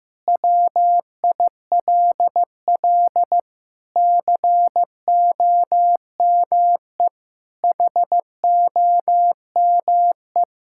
Morse code, heard as a keyed tone, about 15 wpm, 700 Hz. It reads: WILL COME HOME
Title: Sentences from Top 1000 Words - Review Part 5 - 15wpm